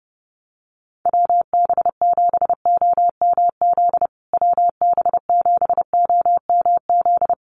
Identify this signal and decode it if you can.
Morse code, heard as a keyed tone, about 30 wpm, 700 Hz. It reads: WB7OMZ WB7OMZ
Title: USA Call Signs Repeated 2x 30wpm